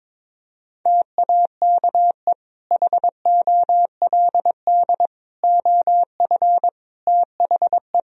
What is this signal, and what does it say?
Morse code, heard as a keyed tone, about 22 wpm, 700 Hz. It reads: TAKE HOLD OF THE
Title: Sentences from Top 1000 Words - Review Part 3 - 22wpm